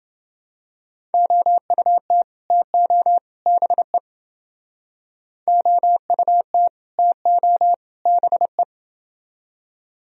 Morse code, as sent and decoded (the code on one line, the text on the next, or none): --- ..- - / - --- / -... / --- ..- - / - --- / -... .
OUT TO BE OUT TO BE